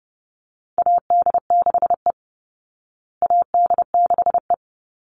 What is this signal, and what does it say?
Morse code, heard as a keyed tone, about 29 wpm, 700 Hz. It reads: AD6E AD6E
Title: Call Signs Repeated 3x - 30wpm